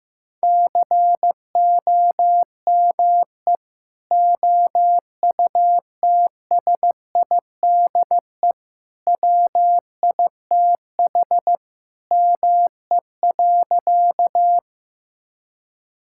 Morse code, heard as a keyed tone, about 15 wpm, 700 Hz. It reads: COME OUTSIDE WITH ME.
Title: Sentences from Top 1000 Words - Review Part 6 - 15wpm